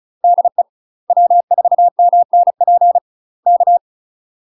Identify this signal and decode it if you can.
Morse code, heard as a keyed tone, about 35 wpm, 700 Hz. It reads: DE W4MNP K
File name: UK MASTER QSOs TRAINING / 35WPM / UK MASTER QSOs 0034